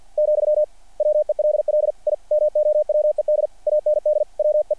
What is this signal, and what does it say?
Morse code, heard as a keyed tone, about 50 wpm, 600 Hz. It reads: , WELL I MOWED AND WE
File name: archived / CRSnet